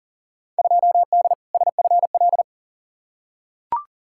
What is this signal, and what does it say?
Morse code, heard as a keyed tone, about 40 wpm, 700 Hz. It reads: 2D SFL
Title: Field Day Exchanges Encore - 40wpm